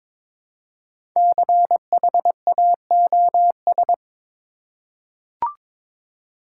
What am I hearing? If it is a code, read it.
Morse code, heard as a keyed tone, about 22 wpm, 700 Hz. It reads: CHAOS
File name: 5 Letter Words - First 3 Letters Given - 22wpm